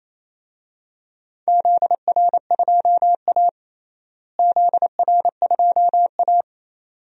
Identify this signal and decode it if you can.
Morse code, heard as a keyed tone, about 28 wpm, 700 Hz. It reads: ZR2A ZR2A
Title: DXCC-100 Call Signs - 4 Characters - Repeated 2x - 28wpm